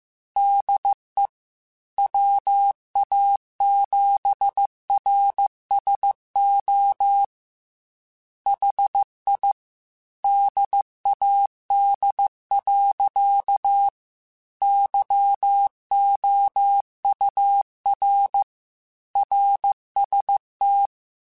Morse code, heard as a keyed tone, about 15 wpm, 800 Hz. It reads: DE WA7RSO HI DAD. YOUR RST
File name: QSOs / QSOs-mp3s / WD0BGP 2@13